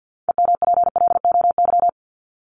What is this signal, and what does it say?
Morse code, heard as a keyed tone, about 52 wpm, 700 Hz. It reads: EMPLOY
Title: ED Suffix - Top 100 Words with Base Words 50wpm